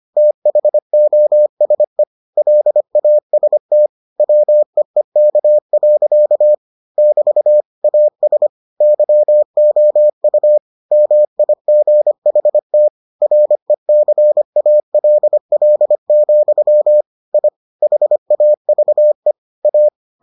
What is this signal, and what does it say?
Morse code, heard as a keyed tone, about 25 wpm, 600 Hz. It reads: THOSE LAST WEEK. <BT> AS YOU MIGHT RECALL, I HAVE A